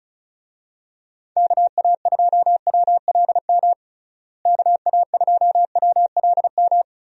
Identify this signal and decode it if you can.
Morse code, heard as a keyed tone, about 36 wpm, 700 Hz. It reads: KA2WLM KA2WLM
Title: USA Call Signs - 6 Characters - Repeated 3x - 35wpm